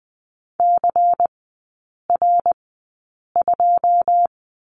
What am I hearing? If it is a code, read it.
Morse code, heard as a keyed tone, about 20 wpm, spaced as for 8 wpm, 690 Hz. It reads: CR2
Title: 093 Lesson 16e - Letter c - Three Characters 20wpm